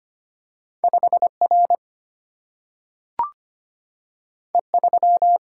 Morse code, heard as a keyed tone, about 25 wpm, 700 Hz. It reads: 5 R E 3
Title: Single-Letter Single-Number 25wpm